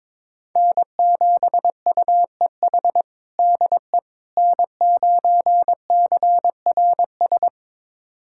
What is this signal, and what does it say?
Morse code, heard as a keyed tone, about 21 wpm, 700 Hz. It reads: N7UEH DE N9CRS
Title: USA Callsign DE Callsign - 5 Characters - 22wpm